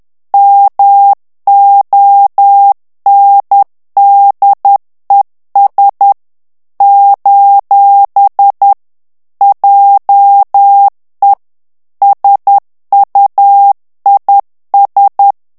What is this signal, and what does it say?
Morse code, heard as a keyed tone, about 10 wpm, 800 Hz. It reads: MONDES : JE SUIS
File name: CW / FICHIERS-MP3 / FRANCAIS TEXTE CLAIR 2 VITESSE 10 MOTS MINUTES